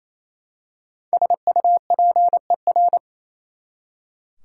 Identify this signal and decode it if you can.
Morse code, heard as a keyed tone, about 28 wpm, 700 Hz. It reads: SUPER
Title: Prefixes and Suffixes 28wpm